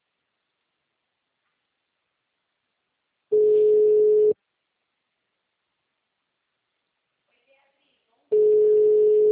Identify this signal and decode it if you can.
Ringback tone (Europe, 425 Hz), with speech.
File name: out- -2002- - - .0